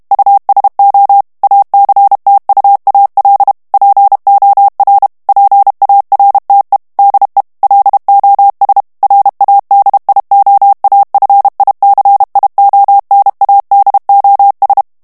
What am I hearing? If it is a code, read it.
Morse code, heard as a keyed tone, about 32 wpm, 800 Hz. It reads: USO ACTUAL POR PARTE DE LOS RADIOAFICIONADOS
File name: morse mp3 / ESPAGNOL TEXTE EN CLAIR 1 VITESSE 30 MOTS MINUTES